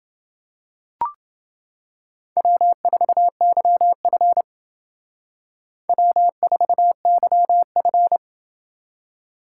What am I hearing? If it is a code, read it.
Morse code, heard as a keyed tone, about 29 wpm, 700 Hz. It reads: W4YF W4YF